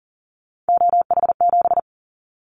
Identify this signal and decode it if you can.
Morse code, heard as a keyed tone, about 40 wpm, 700 Hz. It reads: OH7
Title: Callsign Prefixes for Top 25 DXCC - 40wpm